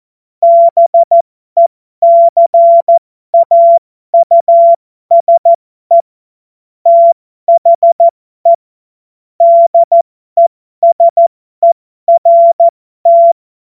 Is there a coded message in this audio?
Yes: Morse code from 0.4 to 13.3 s, heard as a keyed tone, about 14 wpm, spaced as for 12 wpm, 680 Hz. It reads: BECAUSE THE DESERT